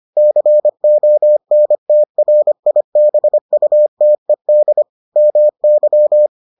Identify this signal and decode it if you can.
Morse code, heard as a keyed tone, about 25 wpm, 600 Hz. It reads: CONTRIBUTED MY